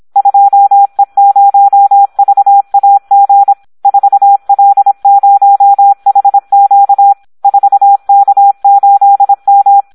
Morse code, heard as a keyed tone, about 26 wpm, 800 Hz, with radio noise. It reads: 2E0VAG 4L0HQ 4K8M
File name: Audio practica CW / SD 1 26